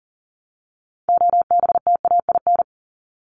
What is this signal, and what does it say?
Morse code, heard as a keyed tone, about 40 wpm, 700 Hz. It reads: OBTAIN
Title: ED Suffix - Top 100 Words with Base Words 40wpm